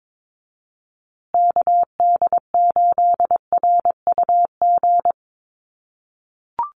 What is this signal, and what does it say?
Morse code, heard as a keyed tone, about 22 wpm, 700 Hz. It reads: KD8RUG
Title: USA Call Signs Repeated 2x 22wpm